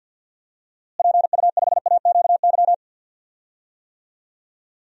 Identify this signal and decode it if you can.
Morse code, heard as a keyed tone, about 48 wpm, 700 Hz. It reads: PU5AQY
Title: DXCC-100 Call Signs - 6 Characters - Repeated 2x - 50wpm